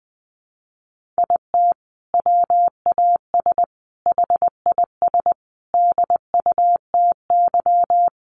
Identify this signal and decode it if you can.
Morse code, heard as a keyed tone, about 20 wpm, 700 Hz. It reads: IT WAS HIS DUTY
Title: Sentences from Top 1500 Words 20wpm